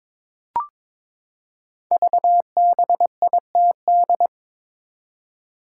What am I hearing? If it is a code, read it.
Morse code, heard as a keyed tone, about 22 wpm, 700 Hz. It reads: VBITD
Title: Code Groups - 5 Letters - 22wpm